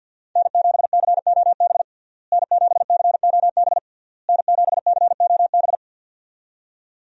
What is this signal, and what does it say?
Morse code, heard as a keyed tone, about 51 wpm, 700 Hz. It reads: N7COB N7COB N7COB
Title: USA Call Signs - 5 Characters - Repeated 3x - 50wpm